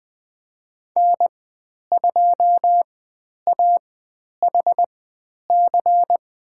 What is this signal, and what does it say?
Morse code, heard as a keyed tone, about 20 wpm, spaced as for 10 wpm, 700 Hz. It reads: N2AHC
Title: 109 Lesson 18g - Letter m - Call Signs 20wpm